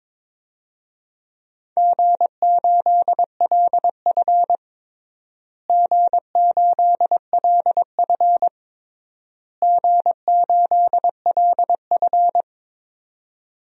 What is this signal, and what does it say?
Morse code, heard as a keyed tone, about 21 wpm, 700 Hz. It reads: G8LF G8LF G8LF
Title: DXCC-100 Call Signs - 4 Characters - Repeated 3x - 22wpm